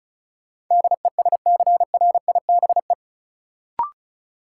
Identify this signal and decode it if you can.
Morse code, heard as a keyed tone, about 36 wpm, 700 Hz. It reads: DESCRIBE
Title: Sentences from Top 1000 Words - New Word Drill - 35wpm